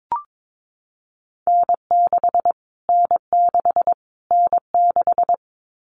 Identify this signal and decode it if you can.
Morse code, heard as a keyed tone, about 22 wpm, 700 Hz. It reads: N6 N6 N6
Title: Callsign Prefixes for Top 25 DXCC - Repeated 3x - 22wpm